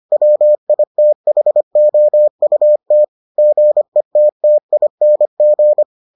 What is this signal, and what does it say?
Morse code, heard as a keyed tone, about 25 wpm, 600 Hz. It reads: WITHOUT GETTING